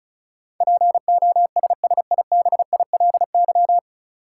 Morse code, heard as a keyed tone, about 36 wpm, 700 Hz. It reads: POSSIBILY
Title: Sentences from Top 1300 Words - New Word Drill Alt - 35wpm